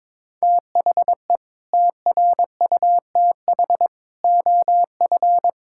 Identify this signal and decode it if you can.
Morse code, heard as a keyed tone, about 22 wpm, 700 Hz. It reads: THE TRUTH OF
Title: Sentences from Top 1600 Words 22wpm